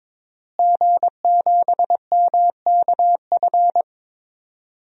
Morse code, heard as a keyed tone, about 22 wpm, 700 Hz. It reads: G7MKF
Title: DXCC-100 Call Signs - 5 Characters - Repeated 2x - 22wpm